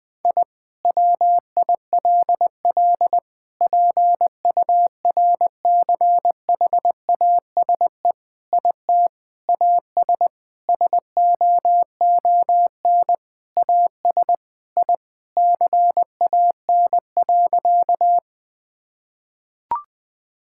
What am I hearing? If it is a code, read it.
Morse code, heard as a keyed tone, about 21 wpm, 700 Hz. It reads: I WILL PURCHASE IT AS SOON AS I CAN.
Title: Sentences from Top 1000 Words - Review Part 5 - 20wpm